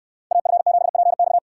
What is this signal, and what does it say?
Morse code, heard as a keyed tone, about 69 wpm, 690 Hz. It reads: AF7PX